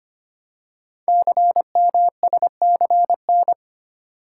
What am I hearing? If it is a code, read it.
Morse code, heard as a keyed tone, about 25 wpm, 700 Hz. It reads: CMSCN
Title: Code Groups - 5 Letters - 25wpm